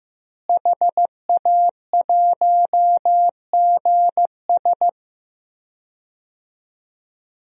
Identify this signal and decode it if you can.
Morse code, heard as a keyed tone, about 15 wpm, 700 Hz. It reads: HA1GS
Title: DXCC-100 Call Signs - 5 Characters - Repeated 2x - 15wpm